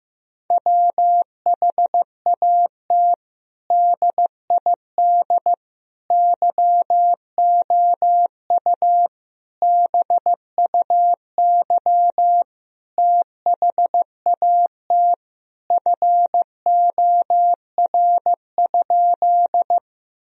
Morse code, heard as a keyed tone, about 15 wpm, 700 Hz. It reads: WHAT DID YOU BUY THAT FOR?
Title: Sentences from Top 1000 Words - Review Part 4 - 15wpm